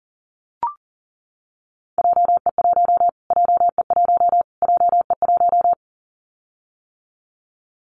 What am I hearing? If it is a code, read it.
Morse code, heard as a keyed tone, about 40 wpm, 700 Hz. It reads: JE1 JE1 JE1
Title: Callsign Prefixes for Top 25 DXCC - Repeated 3x - 40wpm